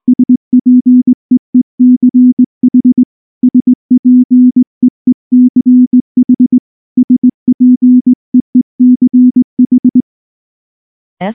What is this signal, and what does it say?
Morse code, heard as a keyed tone, about 19 wpm, 250 Hz. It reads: SPEECH SPEECH SPEECH
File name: morse / morse-training-20WPM-2